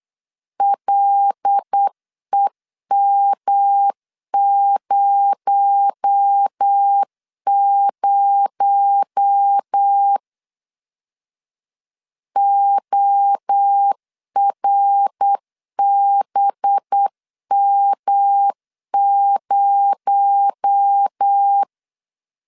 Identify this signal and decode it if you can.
Morse code, heard as a keyed tone, about 8 wpm, 800 Hz. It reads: LEM00 ORBM0
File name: LEZIONI / 6 PKX0 / 6 PKX0 8wpm 3341